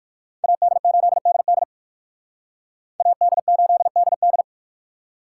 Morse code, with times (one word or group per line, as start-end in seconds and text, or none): 0.4-1.6 AD8DD
3.0-4.4 AD8DD